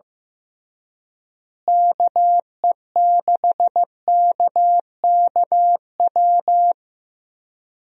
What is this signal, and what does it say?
Morse code, heard as a keyed tone, about 15 wpm, 700 Hz. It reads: KE6KKW
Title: USA Call Signs - 6 Characters - Repeated 3x - 15wpm